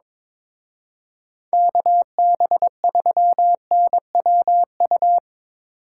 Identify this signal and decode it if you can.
Morse code, heard as a keyed tone, about 24 wpm, 700 Hz. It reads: KB3NWU
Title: Speed Racing - USA 2x3 Call Signs 22wpm